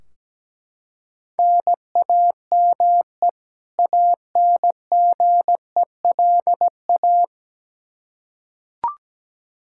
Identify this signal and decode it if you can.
Morse code, heard as a keyed tone, about 17 wpm, 700 Hz. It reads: NAME ANGELA
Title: Practice QSO - Exchange 1 - Beginning - 17wpm